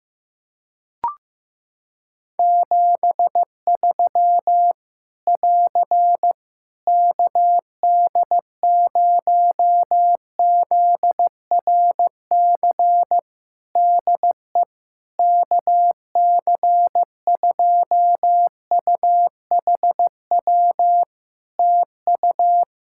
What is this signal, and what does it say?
Morse code, heard as a keyed tone, about 15 wpm, 700 Hz. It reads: 73 <AR> KD0ZRC DE KC2UHW TU
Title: Practice QSO - Exchange 4 - Ending - 15wpm